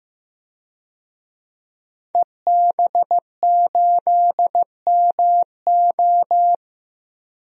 Morse code, heard as a keyed tone, about 15 wpm, 700 Hz. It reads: EB8MO
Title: DXCC-100 Call Signs - 5 Characters - Repeated 2x - 15wpm